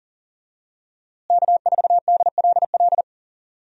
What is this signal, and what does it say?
Morse code, heard as a keyed tone, about 40 wpm, 700 Hz. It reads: K4DLL
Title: USA Call Signs - 5 Characters - Repeated 2x - 40wpm